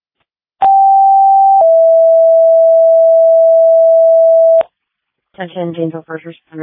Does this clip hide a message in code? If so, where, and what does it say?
Yes: two-tone page from 0.6 to 4.6 s, heard as a one-second tone, then a three-second tone, about 780 then 660 Hz, with speech.